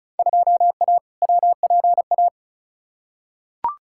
Morse code, heard as a keyed tone, about 36 wpm, 700 Hz. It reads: 2A WPA
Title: Field Day Exchanges - 35wpm